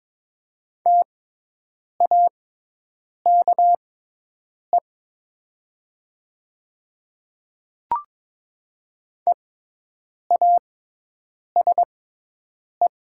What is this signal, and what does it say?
Morse code, heard as a keyed tone, about 22 wpm, spaced as for 8 wpm, 700 Hz. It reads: TAKE EASE
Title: 6x Character Spacing - Top 500 Words - 22wpm